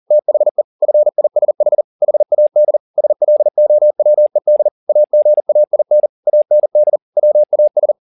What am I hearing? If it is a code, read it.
Morse code, heard as a keyed tone, about 40 wpm, 600 Hz. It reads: THE FISH HAD SLOWED AGAIN AND WAS